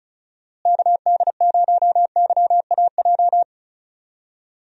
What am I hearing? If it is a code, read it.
Morse code, heard as a keyed tone, about 34 wpm, 700 Hz. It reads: KD0YAJ
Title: USA Call Signs - 6 Characters - Repeated 3x - 35wpm